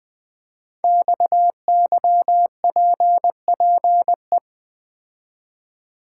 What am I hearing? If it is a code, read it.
Morse code, heard as a keyed tone, about 20 wpm, 700 Hz. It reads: XYPPE
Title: Code Groups - 5 Letters - 20wpm